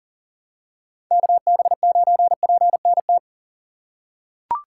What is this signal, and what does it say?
Morse code, heard as a keyed tone, about 40 wpm, 700 Hz. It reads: KB9PNT
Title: USA Call Signs - 6 Characters - 2x Character Spacing - 40wpm